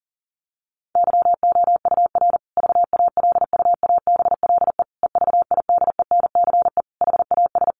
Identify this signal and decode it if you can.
Morse code, heard as a keyed tone, about 40 wpm, 700 Hz. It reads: YOUR VALUABLE EVIDENCE HAS
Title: Sentences from Top 1300 Words 40wpm